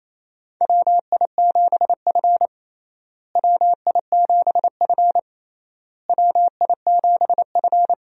Morse code, heard as a keyed tone, about 28 wpm, 700 Hz. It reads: WI7F WI7F WI7F
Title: USA Call Signs - 4 Characters - Repeated 3x - 28wpm